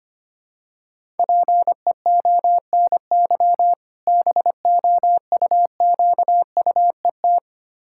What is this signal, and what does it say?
Morse code, heard as a keyed tone, about 25 wpm, 700 Hz. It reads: PEONY BOUQUET
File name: practice / 25wpm / Difficult Words with Hints - 25wpm